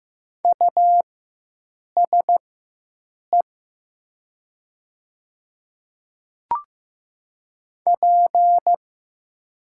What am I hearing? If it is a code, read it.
Morse code, heard as a keyed tone, about 15 wpm, spaced as for 7 wpm, 700 Hz. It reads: USE P
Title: 4x Character Spacing - Top 300 Words - 15wpm